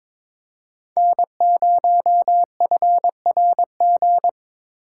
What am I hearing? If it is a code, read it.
Morse code, heard as a keyed tone, about 26 wpm, 700 Hz. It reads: N0FRG
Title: Speed Racing - USA 1x3 Call Signs 22wpm